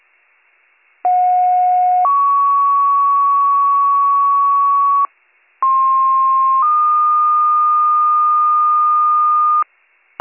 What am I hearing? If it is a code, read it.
Two-tone page, heard as a one-second tone, then a three-second tone, about 730 then 1120 Hz, about 1060 then 1220 Hz. It is heard from 1.0 to 9.6 s.